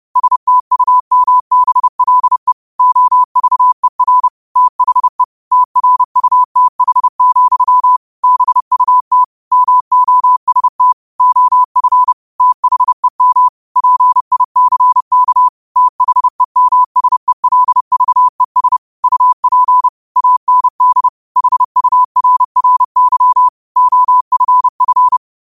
Morse code, heard as a keyed tone, about 30 wpm, 1000 Hz. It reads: STUMBLE OVER THE TRUTH, BUT MOST OF THEM PICK THEMSELVES UP AND HURRY OFF